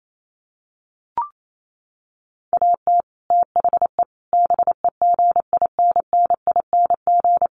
Morse code, heard as a keyed tone, about 28 wpm, 700 Hz. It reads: AT THE BEGINNING
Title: Sets of 5 Words Encore 28wpm